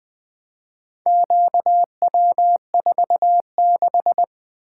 Morse code, heard as a keyed tone, about 20 wpm, 700 Hz. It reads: QW46
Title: Four Letter-Number Combinations - 20wpm